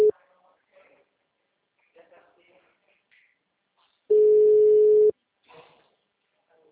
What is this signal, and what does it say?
Ringback tone (Europe, 425 Hz), with speech.
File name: out- -2001- - - .0